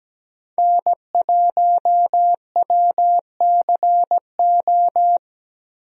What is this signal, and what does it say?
Morse code, heard as a keyed tone, about 17 wpm, 700 Hz. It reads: N1WCO
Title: USA Call Signs - 5 Characters - Repeated 3x - 17wpm